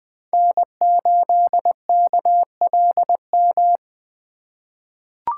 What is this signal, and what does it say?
Morse code, heard as a keyed tone, about 20 wpm, 700 Hz. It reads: N8KLM
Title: USA Call Signs - 5 Characters - 2x Character Spacing - 20wpm